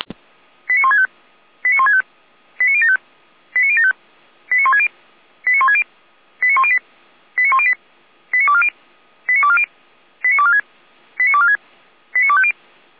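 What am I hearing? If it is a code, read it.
Five-tone selective call (ZVEI), heard as quick runs of five tones, tones of about 80 ms. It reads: 89165 89165 89075 89075 89160 89160 89108 89108 89240 89240 89246 89246 89250
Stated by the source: ZVEI five-tone selective call received on an SDR.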